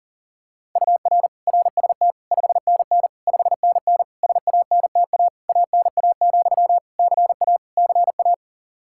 Morse code, heard as a keyed tone, about 40 wpm, 700 Hz. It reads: UR RST 5NN 5NN SANTA ANA, CA CA
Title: Practice QSO - Exchange 1 - Beginning - 40wpm